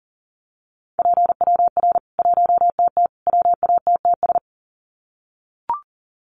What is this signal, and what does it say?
Morse code, heard as a keyed tone, about 40 wpm, 700 Hz. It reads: PWR 1TT WATTS
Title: Common QSO Phrases 40wpm